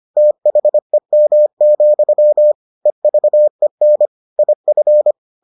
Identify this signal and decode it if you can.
Morse code, heard as a keyed tone, about 25 wpm, 600 Hz. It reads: THEM, EVEN IF